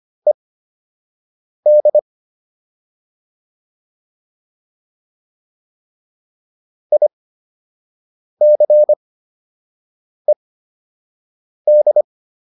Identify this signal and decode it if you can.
Morse code, heard as a keyed tone, about 25 wpm, spaced as for 6 wpm, 600 Hz. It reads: ED ICED